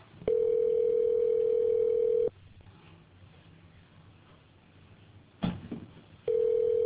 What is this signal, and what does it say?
Ringback tone (North America), 440 + 480 Hz.